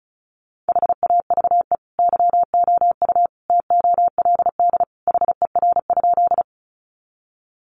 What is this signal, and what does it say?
Morse code, heard as a keyed tone, about 34 wpm, 700 Hz. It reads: HAVE YOU TOLD HER?
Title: Sentences from Top 400 Words 35wpm